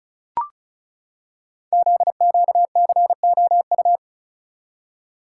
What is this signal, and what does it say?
Morse code, heard as a keyed tone, about 36 wpm, 700 Hz. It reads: ZQCOU
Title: Code Groups - 5 Letters - 35wpm